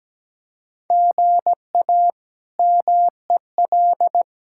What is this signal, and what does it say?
Morse code, heard as a keyed tone, about 17 wpm, 700 Hz. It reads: GA MEL
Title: K1USN SST QSO Practice - 17wpm